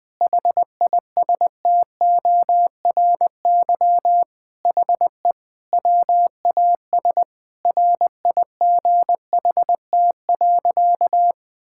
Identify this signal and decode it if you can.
Morse code, heard as a keyed tone, about 19 wpm, 700 Hz. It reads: HISTORY HE WAS RIGHT.
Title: Sentences with 4-Word Phrases - 20wpm